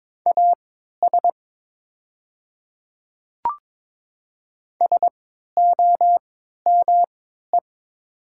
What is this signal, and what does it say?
Morse code, heard as a keyed tone, about 22 wpm, spaced as for 12 wpm, 700 Hz. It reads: AS SOME
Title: 3x Character Spacing - Top 100 Words - 22wpm